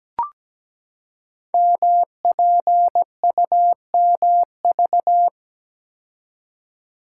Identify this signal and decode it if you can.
Morse code, heard as a keyed tone, about 17 wpm, 700 Hz. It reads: MPUMV
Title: Code Groups - 5 Letters - 17wpm